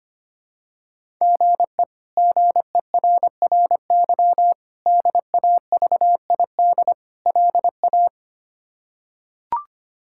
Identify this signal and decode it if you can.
Morse code, heard as a keyed tone, about 25 wpm, 700 Hz. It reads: GE GERRY DAVID LA
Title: K1USN SST QSO Practice - 25wpm